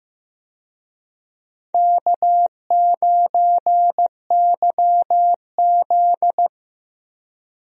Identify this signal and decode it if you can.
Morse code, heard as a keyed tone, about 16 wpm, 700 Hz. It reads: K9YZ
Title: USA Call Signs - 4 Characters - 15wpm